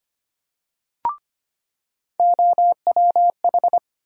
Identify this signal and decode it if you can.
Morse code, heard as a keyed tone, about 25 wpm, 700 Hz. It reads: OWH
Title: USA Call Sign Postfixes - 3 Letters 25wpm